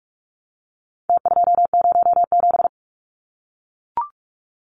Morse code, heard as a keyed tone, about 44 wpm, 700 Hz. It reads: T207
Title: Numbers - Contest Style - 4-Digit 45wpm